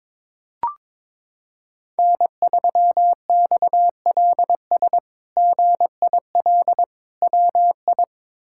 Morse code, heard as a keyed tone, about 22 wpm, 700 Hz. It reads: N3XLS GIL WI
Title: K1USN SST QSO Practice - 22wpm